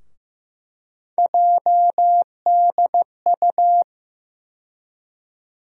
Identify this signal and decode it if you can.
Morse code, heard as a keyed tone, about 15 wpm, 700 Hz. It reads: JDU
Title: USA Call Sign Postfixes - 3 Letters 15wpm